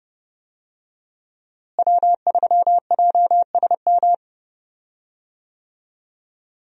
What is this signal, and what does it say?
Morse code, heard as a keyed tone, about 30 wpm, 700 Hz. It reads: W3JSM
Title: USA Call Signs - 5 Characters - Repeated 3x - 30wpm